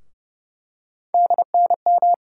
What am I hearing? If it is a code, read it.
Morse code, heard as a keyed tone, about 31 wpm, 700 Hz. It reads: DNM
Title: USA Call Sign Postfixes - 3 Letters 30wpm